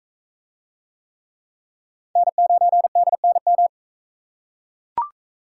Morse code, heard as a keyed tone, about 44 wpm, 700 Hz. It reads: N9DNM